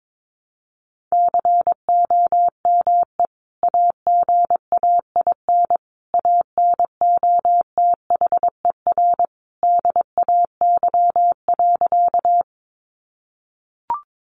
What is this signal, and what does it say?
Morse code, heard as a keyed tone, about 22 wpm, 700 Hz. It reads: COME AGAIN ANOTHER DAY.
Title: Sentences from Top 600 Words 22wpm